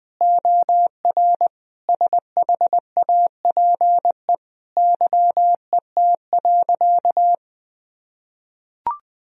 Morse code, heard as a keyed tone, about 19 wpm, 700 Hz. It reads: OR SHAPE YET.
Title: Sentences from Top 1000 Words - Review Part 3 - 20wpm